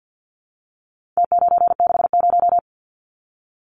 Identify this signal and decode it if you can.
Morse code, heard as a keyed tone, about 50 wpm, 700 Hz. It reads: T960
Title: Numbers - Contest Style - 4-Digit 50wpm